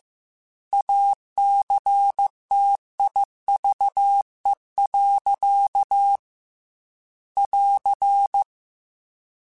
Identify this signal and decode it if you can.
Morse code, heard as a keyed tone, about 15 wpm, 800 Hz. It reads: ACTIVE. <AR>
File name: Technical Subjects / Technical Subjects-mp3s / OR Gate @13